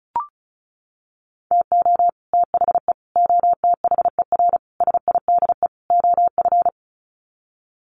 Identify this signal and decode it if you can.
Morse code, heard as a keyed tone, about 34 wpm, 700 Hz. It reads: TO THE OTHER SIDE OF
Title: Sets of 5 Words Encore 35wpm